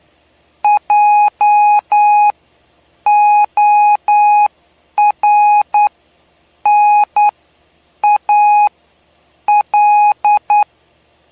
Morse code, heard as a keyed tone, about 9 wpm, spaced as for 7 wpm, 830 Hz. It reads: JORNAL
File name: cw / py2lcd / 07-PY2LCD-Li��o 7